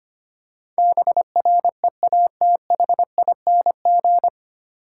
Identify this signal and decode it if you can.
Morse code, heard as a keyed tone, about 25 wpm, 700 Hz. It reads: BREATHING
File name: Nine Letter Words - 25wpm